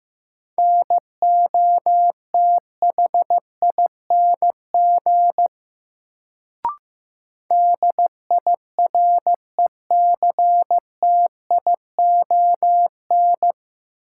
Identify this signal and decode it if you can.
Morse code, heard as a keyed tone, about 15 wpm, 700 Hz. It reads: NOTHING DIRECTION
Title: Sets of 4 Words - New Word Drill - 15wpm